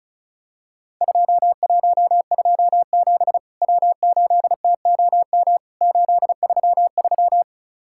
Morse code, heard as a keyed tone, about 36 wpm, 700 Hz. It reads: 2127 W8TOM 833
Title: Worked All Europe DX Contest - QTC Exchange - 35wpm